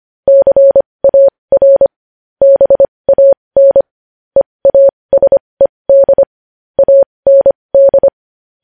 Morse code, heard as a keyed tone, about 25 wpm, spaced as for 20 wpm, 550 Hz. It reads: CAR BAN EASED AND
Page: news headlines in morse code at 20 wpm